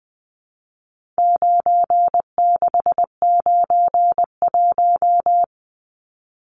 Morse code, heard as a keyed tone, about 20 wpm, 700 Hz. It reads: 9691
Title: Numbers - 4-Digit 20wpm